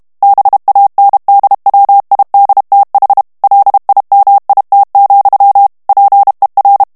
Morse code, heard as a keyed tone, about 32 wpm, 800 Hz. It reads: BANDWIDTH LIMIT, PER
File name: ANGLAIS texte clair 3 vitesse 30 mots minute